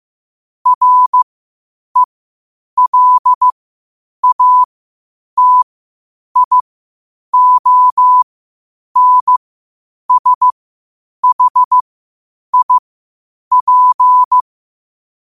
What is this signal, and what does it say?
Morse code, heard as a keyed tone, about 15 wpm, spaced as for 8 wpm, 1000 Hz. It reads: RELATIONSHIP
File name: qotdfiles / 8wpm